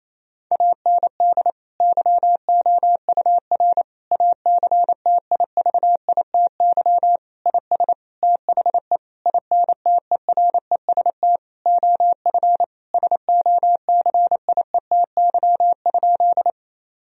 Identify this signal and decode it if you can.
Morse code, heard as a keyed tone, about 28 wpm, 700 Hz. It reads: AND YOUR ACTIVITY IS THE INTEREST OF SOCIETY?
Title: Sentences from Top 1000 Words - Review Part 4 - 28wpm